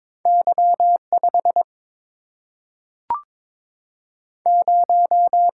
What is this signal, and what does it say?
Morse code, heard as a keyed tone, about 22 wpm, 700 Hz. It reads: Y5 0
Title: Single-Letter Single-Number 22wpm